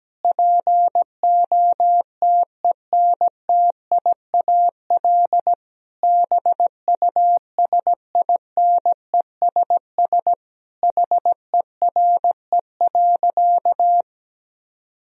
Morse code, heard as a keyed tone, about 17 wpm, 700 Hz. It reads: POTENTIAL BUSINESS HERE.
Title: Sentences from Top 1000 Words - Review Part 4 - 17wpm